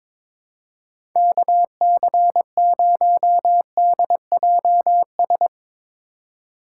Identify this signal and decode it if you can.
Morse code, heard as a keyed tone, about 21 wpm, 700 Hz. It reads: KC0DJS
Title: USA Call Signs - 6 Characters - 4x Character Spacing - 22wpm